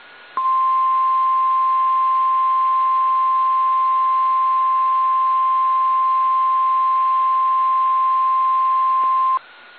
Emergency Alert System (SAME) tones (the 1050 Hz attention tone). It is heard from 0.3 to 9.4 s.
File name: eas / KSDZKDJL / KSDZKDJL